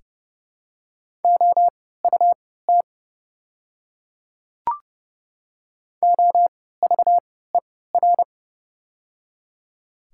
Morse code, heard as a keyed tone, about 30 wpm, spaced as for 17 wpm, 700 Hz. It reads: OUT OVER
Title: 3x Character Spacing - Top 100 Words - 30wpm